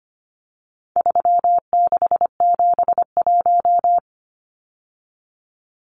Morse code, heard as a keyed tone, about 25 wpm, 700 Hz. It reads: 3671
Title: Numbers - 4-Digit 25wpm